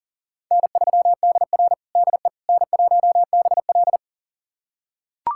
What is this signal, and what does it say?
Morse code, heard as a keyed tone, about 40 wpm, 700 Hz. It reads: N3DR DE N1BL
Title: USA Callsign DE Callsign - 4 Characters - 40wpm